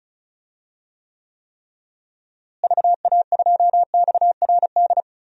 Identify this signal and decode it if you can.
Morse code, heard as a keyed tone, about 36 wpm, 700 Hz. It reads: VA2XRD
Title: DXCC-100 Call Signs - 6 Characters - Repeated 3x - 35wpm